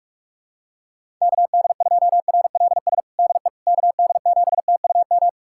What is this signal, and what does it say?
Morse code, heard as a keyed tone, about 46 wpm, 700 Hz. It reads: KD2RLS DE KD7TUM
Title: USA Callsign DE Callsign - 6 Characters - 45wpm